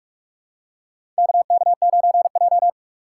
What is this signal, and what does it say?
Morse code, heard as a keyed tone, about 44 wpm, 700 Hz. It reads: KK9J
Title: USA Call Signs - 4 Characters - 2x Character Spacing - 45wpm